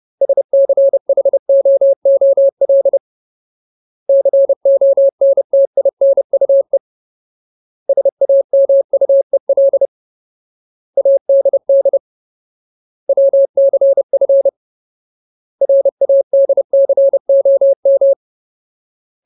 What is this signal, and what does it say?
Morse code, heard as a keyed tone, about 30 wpm, 550 Hz. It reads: SCHOOL CONTINUE SAMUEL ADD WCF RADCOM
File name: Ultimate Word List / 30WPM Ultimate